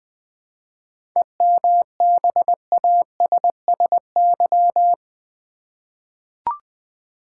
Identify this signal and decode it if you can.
Morse code, heard as a keyed tone, about 21 wpm, 700 Hz. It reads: EMBASSY
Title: 7 Letter Words - First 3 Letters Given - 20wpm